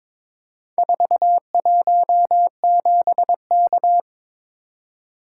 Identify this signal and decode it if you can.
Morse code, heard as a keyed tone, about 22 wpm, 700 Hz. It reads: 417K
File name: practice / 22wpm / Four Letter-Number Combinations - 22wpm